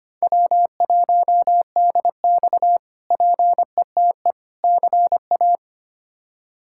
Morse code, heard as a keyed tone, about 25 wpm, 700 Hz. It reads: W1DX PETE CA
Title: K1USN SST QSO Practice - 25wpm